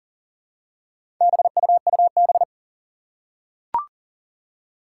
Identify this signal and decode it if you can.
Morse code, heard as a keyed tone, about 37 wpm, 700 Hz. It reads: BUUB
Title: Four Challenging Letter-Number Combinations 40wpm